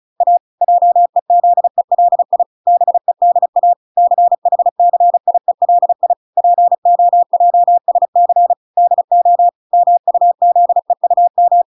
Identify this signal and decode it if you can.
Morse code, heard as a keyed tone, about 34 wpm, 700 Hz. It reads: A JEZELI BEDA CHCIELI POJSC DO MUZEUM